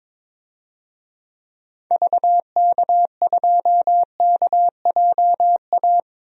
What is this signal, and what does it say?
Morse code, heard as a keyed tone, about 21 wpm, 700 Hz. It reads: VK2KJA
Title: DXCC-100 Call Signs - 6 Characters - 22wpm